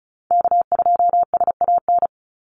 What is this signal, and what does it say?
Morse code, heard as a keyed tone, about 34 wpm, 700 Hz. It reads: K2SAN
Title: Call Signs - 35wpm